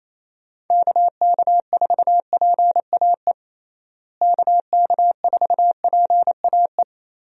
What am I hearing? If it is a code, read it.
Morse code, heard as a keyed tone, about 28 wpm, 700 Hz. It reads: KK4PAE KK4PAE
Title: USA Call Signs - 6 Characters - Repeated 3x - 28wpm